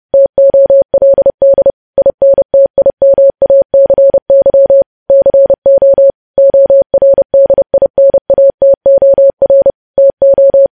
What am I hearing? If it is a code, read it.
Morse code, heard as a keyed tone, about 30 wpm, 550 Hz. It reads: TOLD INTIMACY CO ORDINATOR TO
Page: news headlines in morse code at 30 wpm